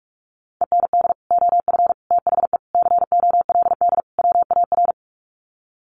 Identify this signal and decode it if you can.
Morse code, heard as a keyed tone, about 44 wpm, 700 Hz. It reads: END OF THE COLD WAR
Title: Sets of 5 Words Encore 45wpm